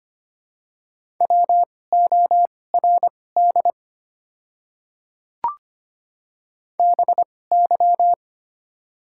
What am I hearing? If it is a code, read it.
Morse code, heard as a keyed tone, about 25 wpm, spaced as for 18 wpm, 690 Hz. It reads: WORD BY
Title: 2x Character Spacing - Top 100 Words - 25wpm